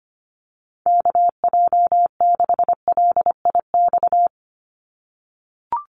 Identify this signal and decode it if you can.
Morse code, heard as a keyed tone, about 25 wpm, 690 Hz. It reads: KJ6LIX
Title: USA Call Signs 25wpm